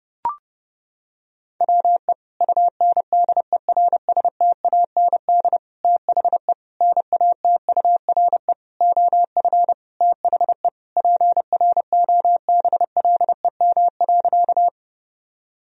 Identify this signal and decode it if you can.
Morse code, heard as a keyed tone, about 32 wpm, 700 Hz. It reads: WE UNDERSTAND THE NATURE OF THE PROBLEM.
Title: Sentences from Top 1000 Words - Review Part 2 - 30wpm